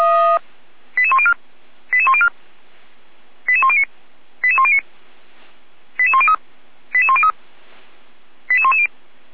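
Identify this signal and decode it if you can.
Five-tone selective call (ZVEI), heard as quick runs of five tones, tones of about 70 ms. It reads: 88104 88104 88109 88109 88103 88103 88110
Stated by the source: ZVEI five-tone selective call received by SDR, fire brigade Orlamünde (feuerwehr-orlamuende), address 88102.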